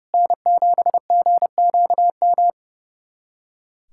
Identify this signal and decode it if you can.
Morse code, heard as a keyed tone, about 29 wpm, 700 Hz. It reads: N7GQM
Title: USA Callsign DE Callsign 30wpm